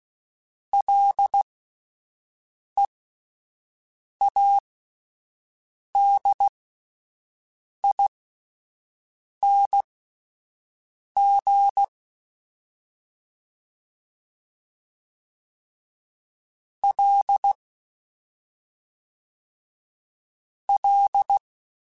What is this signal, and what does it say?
Morse code, heard as a keyed tone, about 16 wpm, spaced as for 5 wpm, 800 Hz. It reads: LEADING L L
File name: Ls / Single L m3s / Single L Prefixe 3x2 Twice @5